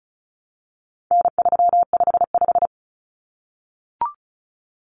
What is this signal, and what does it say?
Morse code, heard as a keyed tone, about 34 wpm, 700 Hz. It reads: N355
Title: Common Cut Numbers - 4-Digit 35wpm